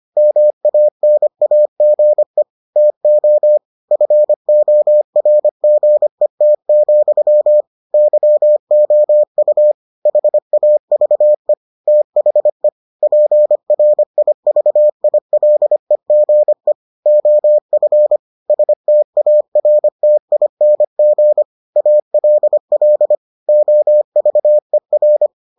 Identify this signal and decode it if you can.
Morse code, heard as a keyed tone, about 25 wpm, 600 Hz. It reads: MANAGE TO FORGET, YOU HAVE THE PRIVILEGE OF STARTING ALL OVER